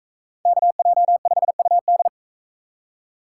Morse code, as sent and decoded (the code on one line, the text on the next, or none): -.- .--- ..... ..- -..
KJ5UD